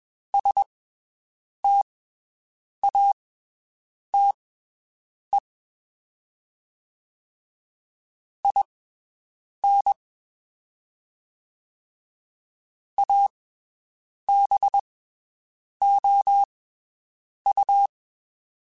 Morse code, heard as a keyed tone, about 22 wpm, spaced as for 7 wpm, 800 Hz. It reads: STATE IN ABOU
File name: (spaced) / QSOs Spaced-mp3s / WD0BGP 1 @20 sp